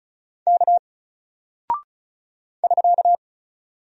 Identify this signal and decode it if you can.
Morse code, heard as a keyed tone, about 35 wpm, 700 Hz. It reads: K <SK>
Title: Single Character (Letter:Number:Punctuation:Prosign) 35wpm